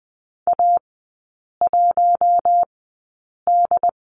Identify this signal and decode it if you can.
Morse code, heard as a keyed tone, about 20 wpm, spaced as for 8 wpm, 700 Hz. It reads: A1D
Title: 074 Lesson 13e - Letter L - Three Characters 20wpm